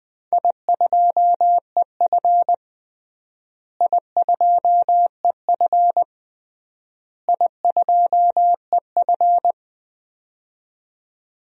Morse code, heard as a keyed tone, about 20 wpm, 700 Hz. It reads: I2EF I2EF I2EF
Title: DXCC-100 Call Signs - 4 Characters - Repeated 3x - 20wpm